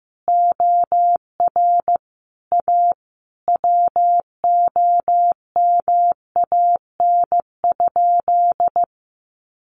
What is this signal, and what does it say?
Morse code, heard as a keyed tone, about 15 wpm, 700 Hz. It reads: OR A WOMAN?
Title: Sentences from Top 1100 Words 15wpm